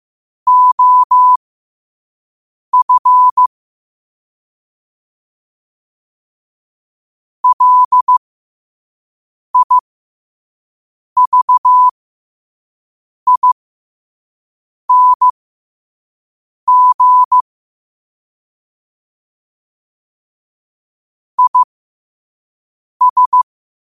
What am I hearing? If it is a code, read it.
Morse code, heard as a keyed tone, about 15 wpm, spaced as for 5 wpm, 1000 Hz. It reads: OF LIVING IS
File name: qotdfiles / 5wpm